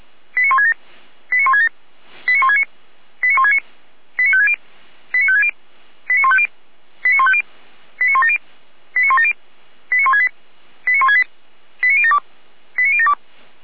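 Five-tone selective call (ZVEI), heard as quick runs of five tones, tones of about 80 ms. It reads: 89167 89167 89169 89169 89560 89560 89150 89150 89170 89170 89168 89168 89072 89072
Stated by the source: ZVEI five-tone selective call received by SDR, from the Feuerwehr Orlamünde audio archive.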